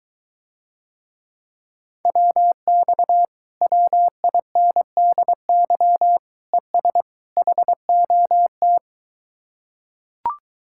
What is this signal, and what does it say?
Morse code, heard as a keyed tone, about 23 wpm, 700 Hz. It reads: WX WINDY ES HOT